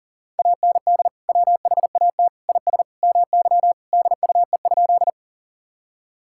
Morse code, heard as a keyed tone, about 40 wpm, 700 Hz. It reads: AND WHAT IS MY DUE?
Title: Sentences from Top 1000 Words - Review Part 6 - 40wpm